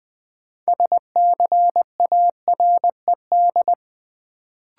Morse code, heard as a keyed tone, about 20 wpm, 700 Hz. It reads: SCARED
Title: Sentences from Top 700 Words - New Word Drill - 20wpm